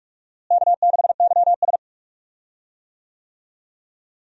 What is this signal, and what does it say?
Morse code, heard as a keyed tone, about 41 wpm, 700 Hz. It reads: K6YS
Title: USA Call Signs - 4 Characters - Repeated 2x - 45wpm